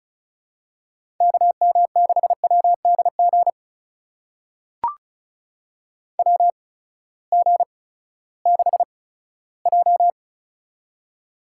Morse code, heard as a keyed tone, about 34 wpm, spaced as for 17 wpm, 700 Hz. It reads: KM6WDG WG6J
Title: USA Call Signs - 8x Character Spacing - 35wpm